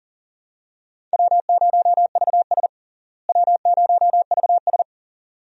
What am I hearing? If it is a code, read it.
Morse code, heard as a keyed tone, about 40 wpm, 700 Hz. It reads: W0VS W0VS
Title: USA Call Signs - 4 Characters - Repeated 2x - 40wpm